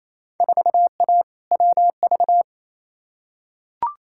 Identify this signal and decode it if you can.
Morse code, heard as a keyed tone, about 28 wpm, 700 Hz. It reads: 4A WV
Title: Field Day Exchanges Encore - 28wpm